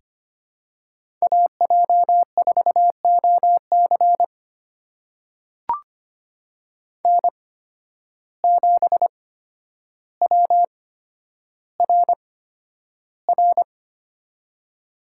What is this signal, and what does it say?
Morse code, heard as a keyed tone, about 25 wpm, spaced as for 11 wpm, 700 Hz. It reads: AJ4OC N7WRR
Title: USA Call Signs - 8x Character Spacing - 25wpm